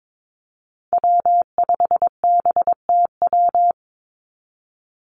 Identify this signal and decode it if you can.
Morse code, heard as a keyed tone, about 22 wpm, 700 Hz. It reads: W5BTW
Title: USA Call Signs 22wpm